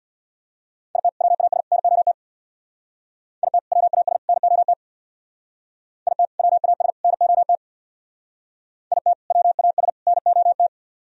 Morse code, heard as a keyed tone, about 65 wpm, 700 Hz. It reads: IT WAS NOT IT WAS NOT IT WAS NOT IT WAS NOT